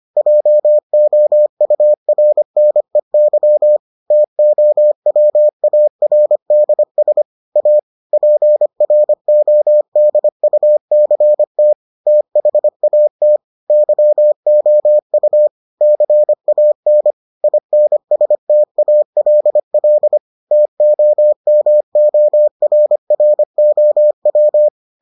Morse code, heard as a keyed tone, about 25 wpm, 600 Hz. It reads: JOURNEY TOWARDS A PRODUCT THAT YOU CAN INSTALL TOMORROW